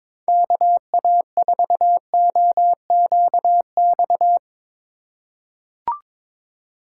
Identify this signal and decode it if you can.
Morse code, heard as a keyed tone, about 22 wpm, 700 Hz. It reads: KA4OQX
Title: USA Call Signs - 8x Character Spacing - 22wpm